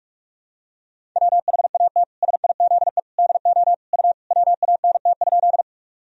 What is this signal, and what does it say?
Morse code, heard as a keyed tone, about 42 wpm, 700 Hz. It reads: WHAT SIZE DO U WANT?
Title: Sentences from Top 400 Words and Abbreviations 45wpm